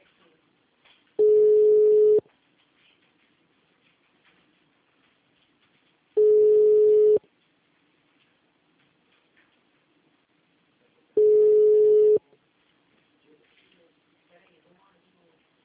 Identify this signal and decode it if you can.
Ringback tone (Europe, 425 Hz), with speech.